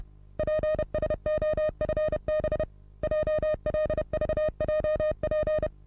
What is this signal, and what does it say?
Morse code, heard as a keyed tone, about 30 wpm, 620 Hz. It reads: PSOFB JLVJP
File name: repos / Morsekurs / DJ4UF Morsen27